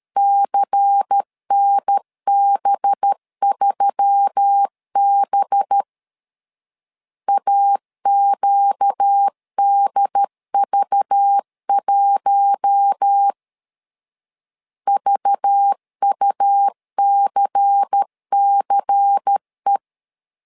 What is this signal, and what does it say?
Morse code, heard as a keyed tone, about 13 wpm, 800 Hz. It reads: CNB3B AQDV1 VUCCE
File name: LEZIONI / 5 LBVC / 5 LBVC 12wpm 1955